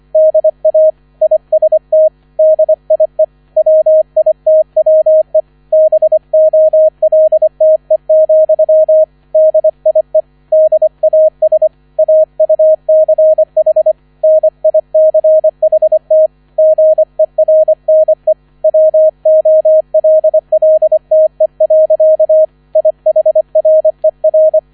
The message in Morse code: DA IST DIE WITWE BOLTE, DIE DAS AUCH NICHT GERNE WOLLTE. IHRER

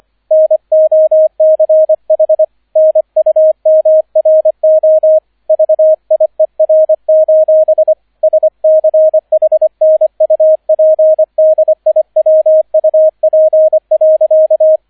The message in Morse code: NOCH NUMRO VIER: SCHNUPDIWUP.

